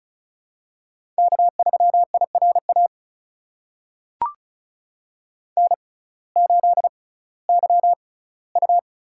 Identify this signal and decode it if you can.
Morse code, heard as a keyed tone, about 34 wpm, spaced as for 19 wpm, 700 Hz. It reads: K3IRA N8YU
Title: USA Call Signs - 5 Characters - 6x Character Spacing - 35wpm